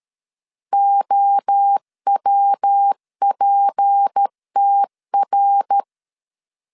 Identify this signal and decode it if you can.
Morse code, heard as a keyed tone, about 13 wpm, 800 Hz. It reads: OWPTR
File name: LEZIONI / 6 PKX0 / 6 P 12wpm 2033